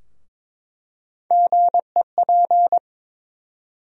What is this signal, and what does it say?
Morse code, heard as a keyed tone, about 21 wpm, 700 Hz. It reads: GEP
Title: USA Call Sign Postfixes - 3 Letters 22wpm